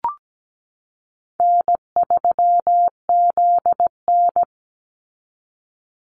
Morse code, heard as a keyed tone, about 17 wpm, 700 Hz. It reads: N3ZN
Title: Easy Call Signs Repeated 3x - 17wpm